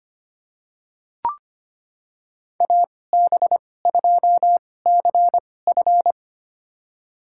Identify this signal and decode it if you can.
Morse code, heard as a keyed tone, about 25 wpm, spaced as for 18 wpm, 700 Hz. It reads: AB2CF
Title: USA Call Signs - 5 Characters - 2x Character Spacing - 25wpm